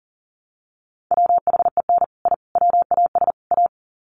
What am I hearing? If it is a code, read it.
Morse code, heard as a keyed tone, about 39 wpm, 700 Hz. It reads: WHEN I WAS A
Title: Sets of 4 Words 40wpm